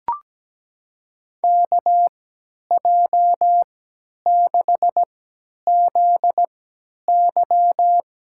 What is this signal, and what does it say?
Morse code, heard as a keyed tone, about 17 wpm, spaced as for 10 wpm, 700 Hz. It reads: KJ6ZY
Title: USA Call Signs - 5 Characters - 3x Character Spacing - 17wpm